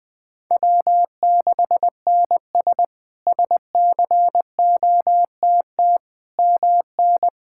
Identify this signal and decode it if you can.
Morse code, heard as a keyed tone, about 21 wpm, 700 Hz. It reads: W6NS SCOTT MN
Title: K1USN SST QSO Practice - 20wpm